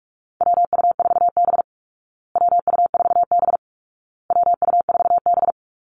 Morse code, heard as a keyed tone, about 44 wpm, 700 Hz. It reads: WU4B WU4B WU4B
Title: Call Signs Repeated 2x - 45wpm